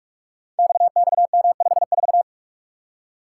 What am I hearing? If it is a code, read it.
Morse code, heard as a keyed tone, about 44 wpm, 700 Hz. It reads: XXM54
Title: Code Groups - 5 Letter:Numbers - 45wpm